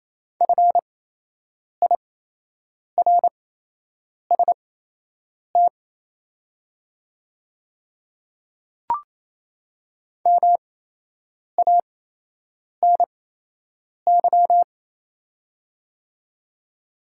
Morse code, heard as a keyed tone, about 28 wpm, spaced as for 8 wpm, 700 Hz. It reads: FIRST MANY
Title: 8x Character Spacing - Top 500 Words - 28wpm